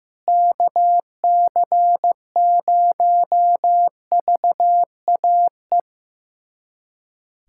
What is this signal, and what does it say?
Morse code, heard as a keyed tone, about 16 wpm, 700 Hz. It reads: KC0VAE
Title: USA Callsign DE Callsign - 6 Characters - 15wpm